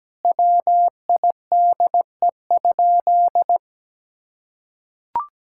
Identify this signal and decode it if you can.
Morse code, heard as a keyed tone, about 17 wpm, 700 Hz. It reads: WIDE?
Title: Sentences from Top 1000 Words - Review Part 6 - 17wpm